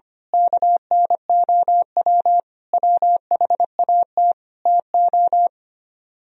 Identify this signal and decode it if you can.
Morse code, heard as a keyed tone, about 26 wpm, 700 Hz. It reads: KNOW WHAT TO
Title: Speed-Racing - Sets of 5 Words 25wpm